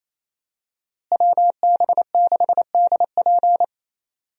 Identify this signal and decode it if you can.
Morse code, heard as a keyed tone, about 28 wpm, 700 Hz. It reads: WB6DP
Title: USA Call Signs - 5 Characters - Repeated 2x - 28wpm